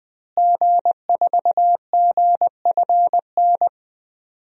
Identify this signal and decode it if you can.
Morse code, heard as a keyed tone, about 21 wpm, 700 Hz. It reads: G4GFN
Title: DXCC-100 Call Signs - 5 Characters - Repeated 3x - 20wpm